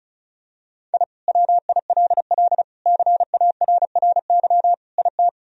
Morse code, heard as a keyed tone, about 34 wpm, 700 Hz. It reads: I WILL CARRY IT
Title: Sentences from Top 1000 Words - Review Part 1 - 35wpm